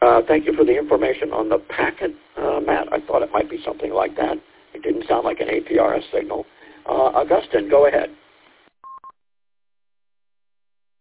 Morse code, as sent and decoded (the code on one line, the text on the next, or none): none